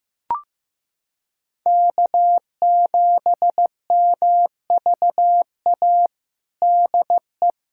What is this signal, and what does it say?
Morse code, heard as a keyed tone, about 15 wpm, 700 Hz. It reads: K7MVA DE
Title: USA Callsign DE Callsign - 5 Characters - 15wpm